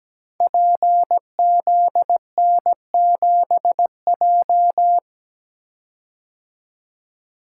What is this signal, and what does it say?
Morse code, heard as a keyed tone, about 17 wpm, 700 Hz. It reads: PZN7J
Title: Code Groups - 5 Letter:Numbers - 17wpm